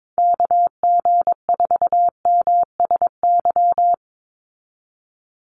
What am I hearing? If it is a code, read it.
Morse code, heard as a keyed tone, about 22 wpm, 700 Hz. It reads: KG4MSY
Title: USA Call Signs 22wpm